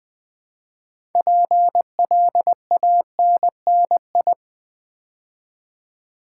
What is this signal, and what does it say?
Morse code, heard as a keyed tone, about 20 wpm, 700 Hz. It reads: PLANNI
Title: Build A Word - 3 Letters At A Time - 8-Letter Words - 20wpm